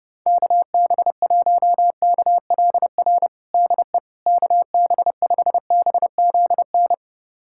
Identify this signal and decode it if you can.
Morse code, heard as a keyed tone, about 30 wpm, 700 Hz. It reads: KB1KLR DE KB5BZN
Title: USA Callsign DE Callsign - 6 Characters - 30wpm